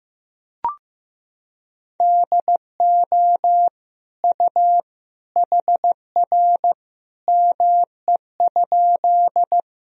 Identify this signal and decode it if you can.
Morse code, heard as a keyed tone, about 15 wpm, 700 Hz. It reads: DO U HR ME?
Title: Sentences from Top 300 Words and Abbreviations 15wpm